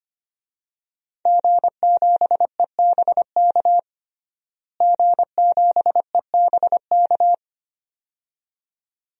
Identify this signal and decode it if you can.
Morse code, heard as a keyed tone, about 25 wpm, 700 Hz. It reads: G7EBK G7EBK
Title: DXCC-100 Call Signs - 5 Characters - Repeated 2x - 25wpm